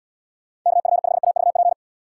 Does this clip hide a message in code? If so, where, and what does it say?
Yes: Morse code from 0.7 to 1.7 s, heard as a keyed tone, about 72 wpm, 700 Hz. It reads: KF5AVP